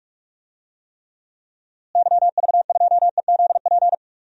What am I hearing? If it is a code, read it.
Morse code, heard as a keyed tone, about 42 wpm, 700 Hz. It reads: YV2EZP
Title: DXCC-100 Call Signs - 6 Characters - Repeated 3x - 45wpm